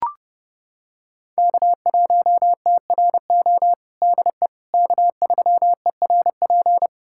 Morse code, heard as a keyed tone, about 32 wpm, 700 Hz. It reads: K1TRO DE K3ERP
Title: USA Callsign DE Callsign 30wpm